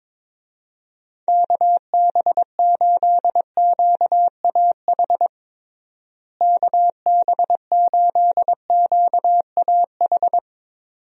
Morse code, heard as a keyed tone, about 22 wpm, 700 Hz. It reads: KB8QAH KB8QAH
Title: USA Call Signs - 6 Characters - Repeated 2x - 22wpm